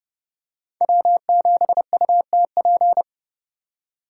Morse code, heard as a keyed tone, about 27 wpm, 700 Hz. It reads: W7UTP